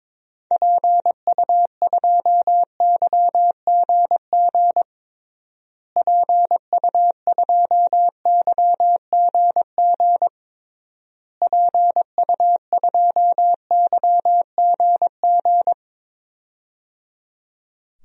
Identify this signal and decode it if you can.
Morse code, heard as a keyed tone, about 22 wpm, 700 Hz. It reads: PU2YGG PU2YGG PU2YGG
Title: DXCC-100 Call Signs - 6 Characters - Repeated 3x - 22wpm